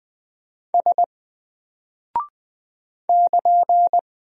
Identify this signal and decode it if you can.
Morse code, heard as a keyed tone, about 20 wpm, 700 Hz. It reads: S <KN>
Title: Single Character (Letter:Number:Punctuation:Prosign) 20wpm